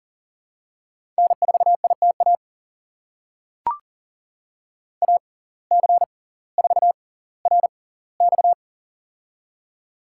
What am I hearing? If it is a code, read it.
Morse code, heard as a keyed tone, about 40 wpm, spaced as for 20 wpm, 700 Hz. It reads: N4ITA AC4RX
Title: USA Call Signs - 5 Characters - 6x Character Spacing - 40wpm